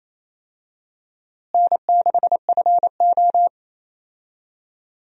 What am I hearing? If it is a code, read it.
Morse code, heard as a keyed tone, about 28 wpm, 700 Hz. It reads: N6FO
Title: USA Call Signs - 4 Characters - Repeated 2x - 28wpm